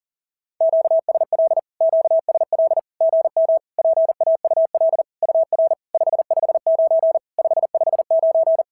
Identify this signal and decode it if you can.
Morse code, heard as a keyed tone, about 40 wpm, 660 Hz. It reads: QSL QSL GM PAUL UR 559 559
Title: SOTA QSO Practice 40wpm